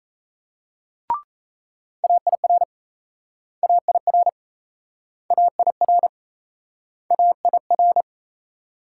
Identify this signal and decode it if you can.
Morse code, heard as a keyed tone, about 40 wpm, 700 Hz. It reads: AIR AIR AIR AIR